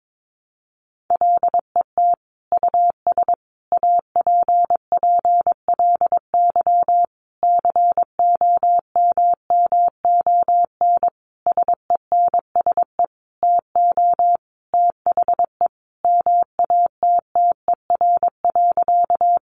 Morse code, heard as a keyed tone, about 22 wpm, 700 Hz. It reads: LET US APPLY COMMON SENSE TO THE MATTER.
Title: Sentences from Top 1200 Words 22wpm